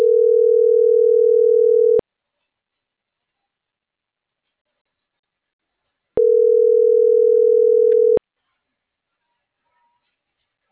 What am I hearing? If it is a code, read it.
Ringback tone (North America), 440 + 480 Hz.